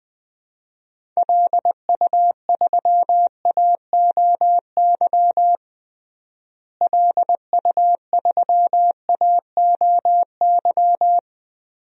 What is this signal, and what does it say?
Morse code, heard as a keyed tone, about 20 wpm, 700 Hz. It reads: LU3AOY LU3AOY
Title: DXCC-100 Call Signs - 6 Characters - Repeated 3x - 20wpm